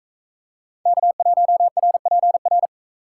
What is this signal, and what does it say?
Morse code, heard as a keyed tone, about 38 wpm, 700 Hz. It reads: K1RPR